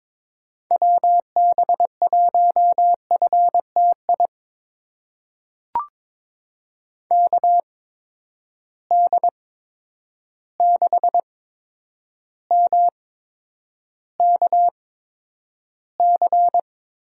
Morse code, heard as a keyed tone, about 22 wpm, spaced as for 9 wpm, 700 Hz. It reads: WB1FTI KD6MKC
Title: USA Call Signs - 6 Characters - 8x Character Spacing - 22wpm